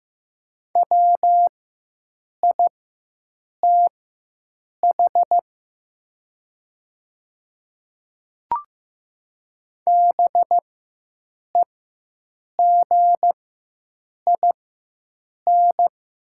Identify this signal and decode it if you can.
Morse code, heard as a keyed tone, about 15 wpm, spaced as for 7 wpm, 700 Hz. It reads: WITH BEGIN
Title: 4x Character Spacing - Top 300 Words - 15wpm